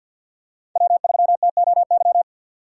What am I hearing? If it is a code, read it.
Morse code, heard as a keyed tone, about 47 wpm, 700 Hz. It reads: W3TOY
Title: USA Call Signs - 5 Characters - 6x Character Spacing - 50wpm